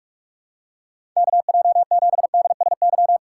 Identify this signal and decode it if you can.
Morse code, heard as a keyed tone, about 45 wpm, 700 Hz. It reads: KJ7DSY
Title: USA Call Signs - 2x Character Spacing - 45wpm